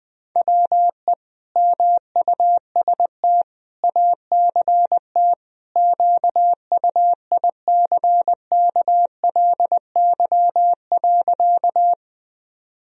Morse code, heard as a keyed tone, about 20 wpm, 700 Hz. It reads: WE MUST ACT QUICKLY.
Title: Sentences from Top 1000 Words - Review Part 1 - 20wpm